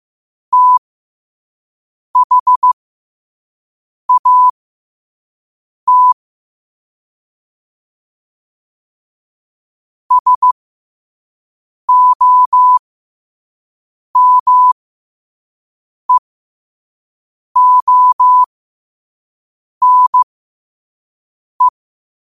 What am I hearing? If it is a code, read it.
Morse code, heard as a keyed tone, about 15 wpm, spaced as for 5 wpm, 1000 Hz. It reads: THAT SOMEONE
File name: qotdfiles / 5wpm